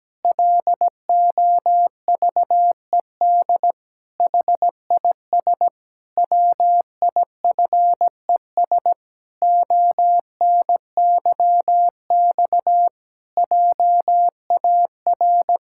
Morse code, heard as a keyed tone, about 17 wpm, 700 Hz. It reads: LOVED HIS WIFES ONYX JAR
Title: Pangram Challenge v2 - 17wpm